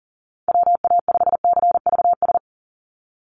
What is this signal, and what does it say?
Morse code, heard as a keyed tone, about 40 wpm, 700 Hz. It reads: WA5CVS
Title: USA Call Signs 40wpm